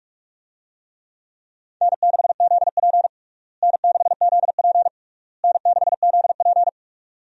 Morse code, heard as a keyed tone, about 46 wpm, 700 Hz. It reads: N6ZP N6ZP N6ZP
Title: USA Call Signs - 4 Characters - Repeated 3x - 45wpm